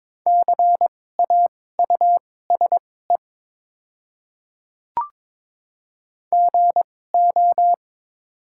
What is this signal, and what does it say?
Morse code, heard as a keyed tone, about 22 wpm, spaced as for 16 wpm, 700 Hz. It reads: CAUSE GO
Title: 2x Character Spacing - Top 200 Words - 22wpm